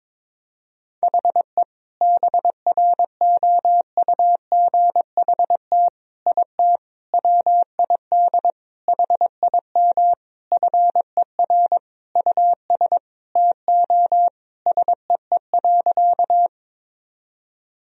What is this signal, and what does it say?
Morse code, heard as a keyed tone, about 22 wpm, 700 Hz. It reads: HE BROUGHT IT WID HIM FER US TO SEE.
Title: Sentences from Top 500 Words and Abbreviations 22wpm